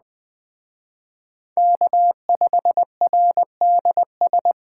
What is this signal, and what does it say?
Morse code, heard as a keyed tone, about 24 wpm, 700 Hz. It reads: K5RDS
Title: Speed Racing - USA 1x3 Call Signs 20wpm